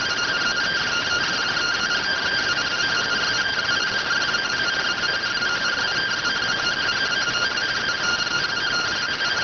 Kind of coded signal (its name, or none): two-tone teleprinter signal